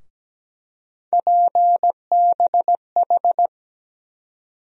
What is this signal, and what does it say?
Morse code, heard as a keyed tone, about 17 wpm, 700 Hz. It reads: PBH
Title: USA Call Sign Postfixes - 3 Letters 17wpm